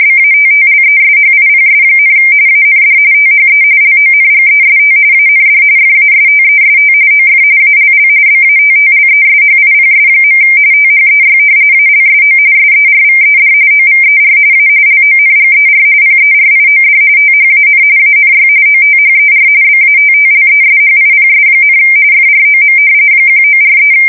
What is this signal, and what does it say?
RTTY, about 45.45 baud. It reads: BATTLESHIPS, AND THE LONGEST AND FASTEST OF ALL BATTLESHIPS. THEY ALSO SERVED OVER A LONGER PERIOD THAN ANY OTHER BATTLESHIPS, FROM 1943
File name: fsk170 / battleshipnj